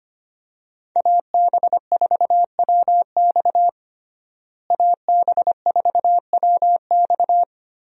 Morse code, heard as a keyed tone, about 25 wpm, 700 Hz. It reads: AB4WX AB4WX
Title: USA Call Signs - 5 Characters - Repeated 3x - 25wpm